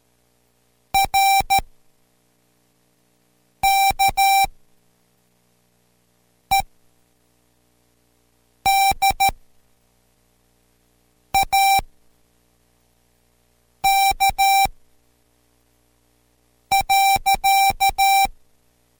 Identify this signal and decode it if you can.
Morse code, heard as a keyed tone, about 13 wpm, spaced as for 4 wpm, 780 Hz. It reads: RKED A K.